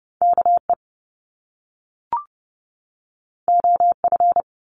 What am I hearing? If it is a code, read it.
Morse code, heard as a keyed tone, about 30 wpm, 700 Hz. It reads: KE OF
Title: Top Two Letter Combinations in Words 30wpm